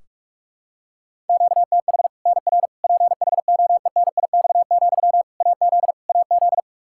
Morse code, heard as a keyed tone, about 46 wpm, 700 Hz. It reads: QTH NR PHOENIX, AZ AZ
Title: Practice QSO - Exchange 1 - Beginning - 45wpm